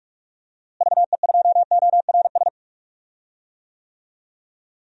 Morse code, heard as a keyed tone, about 46 wpm, 700 Hz. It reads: VE2ORS
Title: DXCC-100 Call Signs - 6 Characters - Repeated 3x - 45wpm